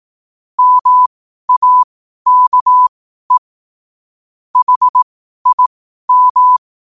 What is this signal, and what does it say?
Morse code, heard as a keyed tone, about 18 wpm, spaced as for 12 wpm, 1000 Hz. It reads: MAKE HIM
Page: 12 wpm morse code quotes